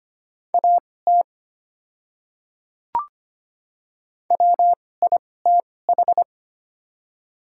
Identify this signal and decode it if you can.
Morse code, heard as a keyed tone, about 25 wpm, spaced as for 18 wpm, 690 Hz. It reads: AT WITH
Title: 2x Character Spacing - Top 100 Words - 25wpm